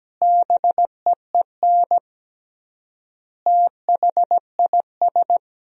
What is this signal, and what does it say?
Morse code, heard as a keyed tone, about 17 wpm, 700 Hz. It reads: BEEN THIS
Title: 3x Word-Spacing - Sentences from Top 300 Words - 17wpm